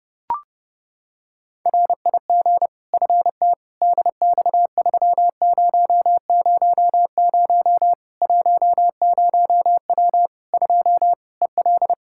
Morse code, heard as a keyed tone, about 30 wpm, 700 Hz. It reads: RIG FT DX3000 10W 2 EL
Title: Practice QSO - Exchange 2 - Info - 30wpm